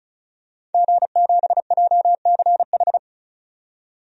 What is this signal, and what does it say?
Morse code, heard as a keyed tone, about 34 wpm, 700 Hz. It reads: G7JCH
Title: DXCC-100 Call Signs - 5 Characters - 35wpm